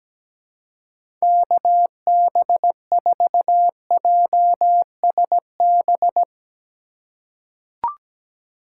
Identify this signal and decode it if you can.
Morse code, heard as a keyed tone, about 17 wpm, 700 Hz. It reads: KB4JSB
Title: USA Call Signs - 4x Character Spacing - 17wpm